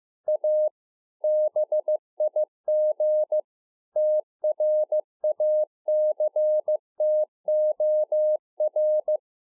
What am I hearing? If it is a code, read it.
Morse code, heard as a keyed tone, about 15 wpm, 610 Hz. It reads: A BIG TRACTOR